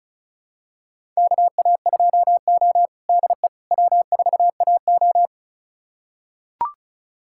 Morse code, heard as a keyed tone, about 34 wpm, 700 Hz. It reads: KA2O DE W4AO
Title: USA Callsign DE Callsign - 4 Characters - 35wpm